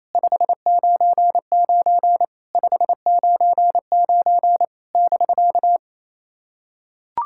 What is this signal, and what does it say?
Morse code, heard as a keyed tone, about 28 wpm, 700 Hz. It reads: 599 599 <BK>
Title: SOTA QSO Practice 28wpm